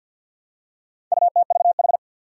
Morse code, heard as a keyed tone, about 51 wpm, 700 Hz. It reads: UTVH
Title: Four Challenging Letter-Number Combinations 50wpm